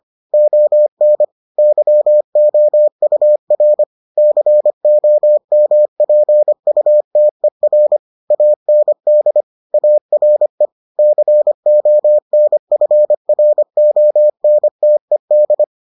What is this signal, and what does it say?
Morse code, heard as a keyed tone, about 25 wpm, 600 Hz. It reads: ON YOUR COMPUTER AND ARE CONFRONTED